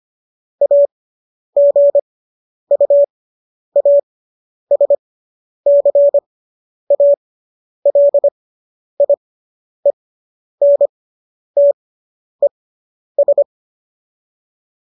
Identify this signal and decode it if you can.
Morse code, heard as a keyed tone, about 25 wpm, spaced as for 10 wpm, 570 Hz. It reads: AGUASCALIENTES